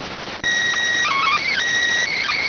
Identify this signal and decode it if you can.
SSTV (slow-scan television), PD 290.